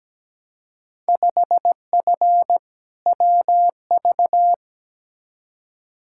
Morse code, heard as a keyed tone, about 17 wpm, 700 Hz. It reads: 5F WV
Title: Field Day Exchanges Encore - 17wpm